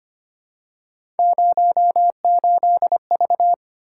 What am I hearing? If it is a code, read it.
Morse code, heard as a keyed tone, about 25 wpm, 700 Hz. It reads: 08V
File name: Three Letter-Number Combinations - 25wpm